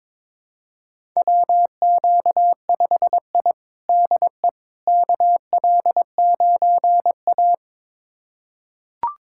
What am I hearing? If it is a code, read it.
Morse code, heard as a keyed tone, about 22 wpm, 700 Hz. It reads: WQ5I DE KL9A
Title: USA Callsign DE Callsign - 4 Characters - 22wpm